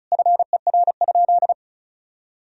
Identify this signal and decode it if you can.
Morse code, heard as a keyed tone, about 36 wpm, 700 Hz. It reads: FER?
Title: Sentences from Top 400 Words and Abbreviations 35wpm